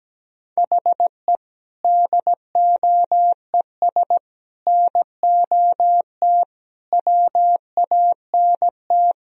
Morse code, heard as a keyed tone, about 17 wpm, 700 Hz. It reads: HE DOES NOT WANT
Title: Speed-Racing - Sets of 5 Words 17wpm